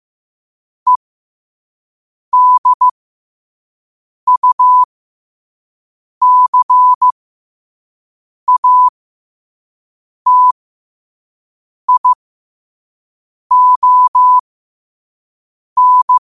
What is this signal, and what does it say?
Morse code, heard as a keyed tone, about 15 wpm, spaced as for 5 wpm, 1000 Hz. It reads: EDUCATION
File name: qotdfiles / 5wpm